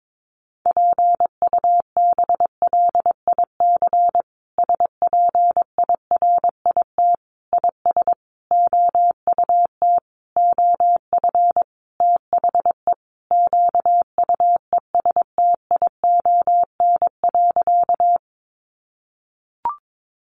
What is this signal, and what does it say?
Morse code, heard as a keyed tone, about 22 wpm, 700 Hz. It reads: PUBLIC SPIRIT IS OUT OF THE QUESTION.
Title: Sentences from Top 1400 Words 22wpm